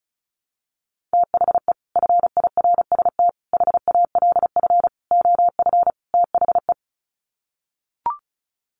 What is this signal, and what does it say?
Morse code, heard as a keyed tone, about 34 wpm, 700 Hz. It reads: THE FIRST HALF OF THE
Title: Sets of 5 Words 35wpm